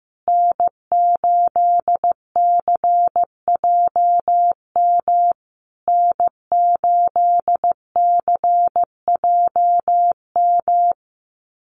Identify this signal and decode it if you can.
Morse code, heard as a keyed tone, about 15 wpm, 700 Hz. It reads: N8CJM N8CJM
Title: USA Call Signs Repeated 3x 15wpm